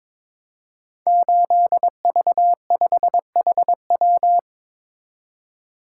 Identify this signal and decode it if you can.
Morse code, heard as a keyed tone, about 22 wpm, 700 Hz. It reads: 8V5HW
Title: Code Groups - 5 Letter:Numbers - 22wpm